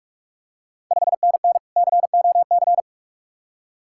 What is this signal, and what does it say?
Morse code, heard as a keyed tone, about 45 wpm, 700 Hz. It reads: 5NN COC
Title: Ontario QSO Party Exchange 45wpm